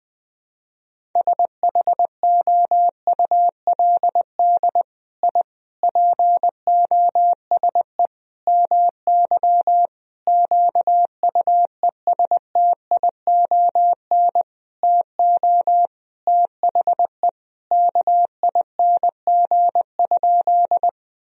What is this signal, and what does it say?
Morse code, heard as a keyed tone, about 21 wpm, 700 Hz. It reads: SHOULD I POSE MY QUESTION TO THE KING?
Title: Sentences from Top 1000 Words - Review Part 2 - 20wpm